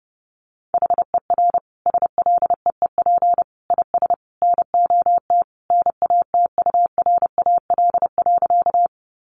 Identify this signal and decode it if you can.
Morse code, heard as a keyed tone, about 30 wpm, 700 Hz. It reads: HER SLEEP IS NOT NATURAL.
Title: Sentences from Top 700 Words 30wpm